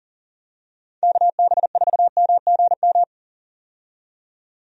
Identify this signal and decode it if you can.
Morse code, heard as a keyed tone, about 40 wpm, 700 Hz. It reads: KB4MGM
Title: USA Call Signs - 6 Characters - 40wpm